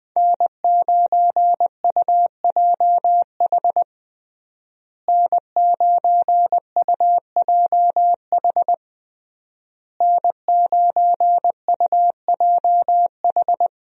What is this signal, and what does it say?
Morse code, heard as a keyed tone, about 20 wpm, 700 Hz. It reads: N9UJH N9UJH N9UJH
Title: USA Call Signs - 5 Characters - Repeated 3x - 20wpm